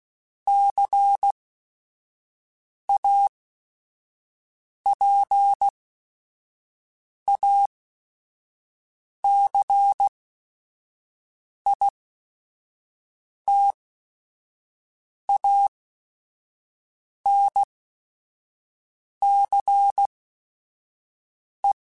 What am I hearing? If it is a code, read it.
Morse code, heard as a keyed tone, about 16 wpm, spaced as for 5 wpm, 800 Hz. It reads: CAPACITANCE
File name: Technical Subjects / Technical Subjects-mp3s / XC @5